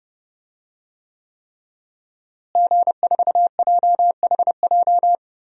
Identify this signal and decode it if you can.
Morse code, heard as a keyed tone, about 30 wpm, 700 Hz. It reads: G4JHJ
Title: DXCC-100 Call Signs - 5 Characters - Repeated 3x - 30wpm